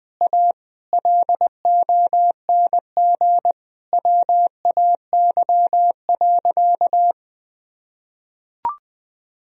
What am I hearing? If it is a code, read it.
Morse code, heard as a keyed tone, about 20 wpm, 700 Hz. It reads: A LONG WAY.
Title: Sentences from Top 1000 Words - Review Part 1 - 20wpm